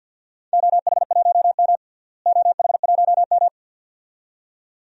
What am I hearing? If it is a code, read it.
Morse code, heard as a keyed tone, about 53 wpm, 700 Hz. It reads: OH1M OH1M
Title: DXCC-100 Call Signs - 4 Characters - Repeated 2x - 50wpm